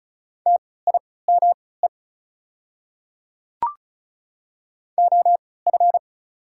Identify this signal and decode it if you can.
Morse code, heard as a keyed tone, about 34 wpm, spaced as for 20 wpm, 700 Hz. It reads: TIME OF
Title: 3x Character Spacing - Top 100 Words - 35wpm